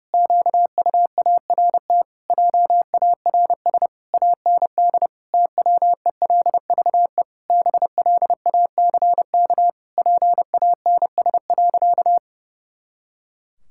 Morse code, heard as a keyed tone, about 30 wpm, 700 Hz. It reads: QUART JARS AND TWELVE BLACK PANS.
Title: Pangram Challenge v2 - 30wpm